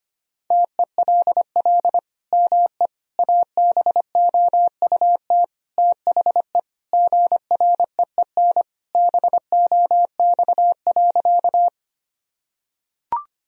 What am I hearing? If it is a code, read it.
Morse code, heard as a keyed tone, about 25 wpm, 700 Hz. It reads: TELL ME ABOUT THE GREEN BOX.
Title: Sentences from Top 1000 Words - Review Part 2 - 25wpm